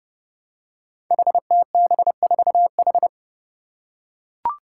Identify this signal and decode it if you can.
Morse code, heard as a keyed tone, about 30 wpm, 700 Hz. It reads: HTB4H
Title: Five Challenging Letter-Number Combinations 30wpm